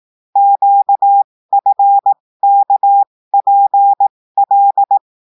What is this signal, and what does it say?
Morse code, heard as a keyed tone, about 18 wpm, spaced as for 15 wpm, 800 Hz. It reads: QFKPL
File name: cours-du-lundi files / cours-1580-texte-chiffre-15-mpm